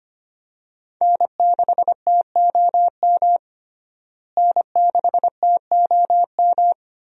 Morse code, heard as a keyed tone, about 25 wpm, 700 Hz. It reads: N6TOM N6TOM
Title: USA Call Signs - 5 Characters - Repeated 2x - 25wpm